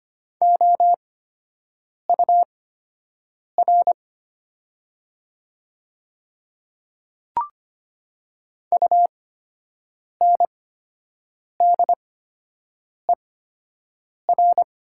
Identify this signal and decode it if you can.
Morse code, heard as a keyed tone, about 25 wpm, spaced as for 7 wpm, 700 Hz. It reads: OUR UNDER
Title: 8x Character Spacing - Top 200 Words - 25wpm